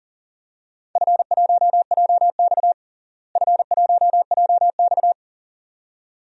Morse code, heard as a keyed tone, about 40 wpm, 700 Hz. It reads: F1JX F1JX
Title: DXCC-100 Call Signs - 4 Characters - Repeated 2x - 40wpm